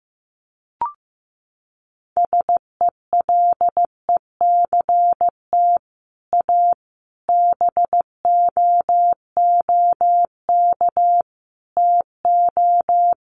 Morse code, heard as a keyed tone, about 14 wpm, 700 Hz. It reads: SELECT A BOOK TO
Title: Sentences from Top 1300 Words 15wpm